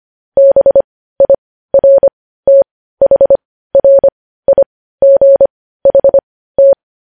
Morse code, heard as a keyed tone, about 25 wpm, spaced as for 15 wpm, 550 Hz. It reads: BIRTHRIGHT